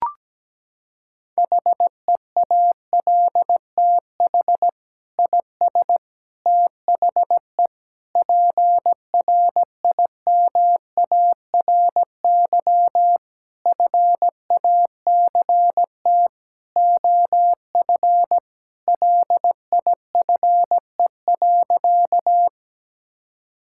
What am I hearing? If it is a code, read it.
Morse code, heard as a keyed tone, about 17 wpm, 700 Hz. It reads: HEALTH IS THE PRIMARY FACT OF LIFE.
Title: Sentences from Top 1000 Words - Review Part 5 - 17wpm